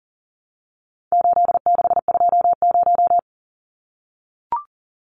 Morse code, heard as a keyed tone, about 40 wpm, 700 Hz. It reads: 8620
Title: Numbers - 4-Digit 40wpm